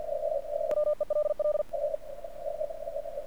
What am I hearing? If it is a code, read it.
Morse code, heard as a keyed tone, about 50 wpm, 1200 Hz, with radio noise. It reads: WELL
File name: archived / CRSnet